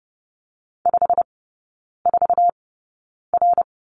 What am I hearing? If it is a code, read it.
Morse code, heard as a keyed tone, about 30 wpm, spaced as for 9 wpm, 700 Hz. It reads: 54R
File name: course / 30:10wpm / 087 Lesson 15e - Number 5 - Three Characters 30wpm